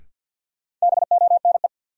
Morse code, heard as a keyed tone, about 53 wpm, 700 Hz. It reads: BONE
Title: Sentences from Top 1600 Words - New Word Drill - 50wpm